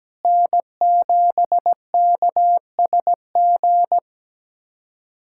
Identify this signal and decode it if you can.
Morse code, heard as a keyed tone, about 17 wpm, 700 Hz. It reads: N7KSG
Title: USA Callsign DE Callsign - 5 Characters - 17wpm